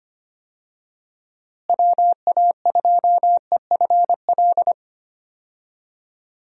Morse code, heard as a keyed tone, about 25 wpm, 700 Hz. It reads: WA2EFL
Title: USA Call Signs - 6 Characters - Repeated 3x - 25wpm